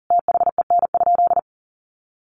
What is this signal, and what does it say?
Morse code, heard as a keyed tone, about 40 wpm, 700 Hz. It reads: THEN?
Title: Sentences from Top 1500 Words 40wpm